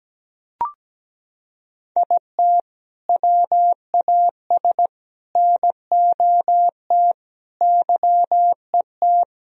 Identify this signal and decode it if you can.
Morse code, heard as a keyed tone, about 17 wpm, 700 Hz. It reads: IT WAS NOT YET
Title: Sentences from Top 1000 Words - Review Part 1 - 17wpm